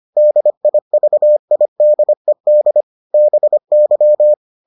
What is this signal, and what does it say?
Morse code, heard as a keyed tone, about 25 wpm, 600 Hz. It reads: DIVIDED BY